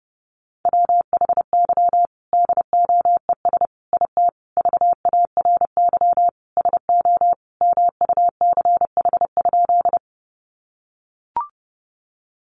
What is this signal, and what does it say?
Morse code, heard as a keyed tone, about 30 wpm, 700 Hz. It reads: WHY DOES IT VARY SO MUCH?
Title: Sentences from Top 1200 Words 30wpm